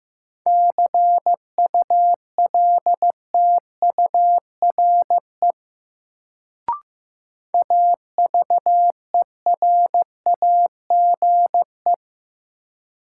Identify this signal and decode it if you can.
Morse code, heard as a keyed tone, about 15 wpm, 700 Hz. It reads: CULTURE AVERAGE
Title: Sentences from Top 1000 Words - Review Part 6 - New Word Drill - 15wpm